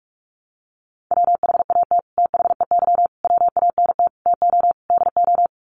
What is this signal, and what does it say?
Morse code, heard as a keyed tone, about 44 wpm, 700 Hz. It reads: WHAT THEY WANT TO DO
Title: Sets of 5 Words Encore2 45wpm